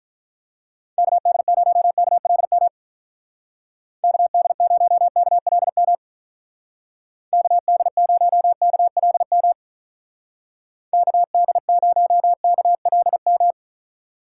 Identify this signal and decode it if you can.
Morse code, heard as a keyed tone, about 43 wpm, 700 Hz. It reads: KD0KLM KD0KLM KD0KLM KD0KLM
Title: Speed Racing - USA 2x3 Call Signs 35wpm